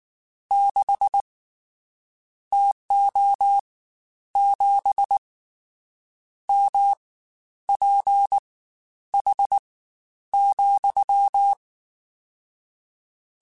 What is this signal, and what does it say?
Morse code, heard as a keyed tone, about 19 wpm, spaced as for 10 wpm, 800 Hz. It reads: 6 TO 7 MPH,
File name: Spaced Characters / Spaced Characters-mp3s / Weather 4 24 sp @18